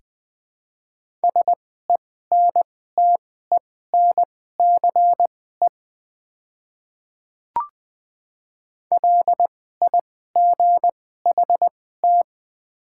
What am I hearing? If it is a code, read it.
Morse code, heard as a keyed tone, about 20 wpm, spaced as for 15 wpm, 700 Hz. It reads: SENTENCE LIGHT
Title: 2x Character Spacing - Top 200 Words - 20wpm